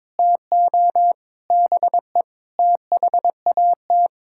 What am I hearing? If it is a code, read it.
Morse code, heard as a keyed tone, about 22 wpm, 700 Hz. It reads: TO BE THAT
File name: practice / 22wpm / Sets of 8 Words - 22wpm